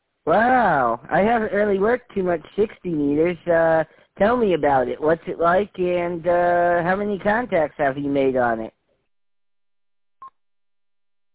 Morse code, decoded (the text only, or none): none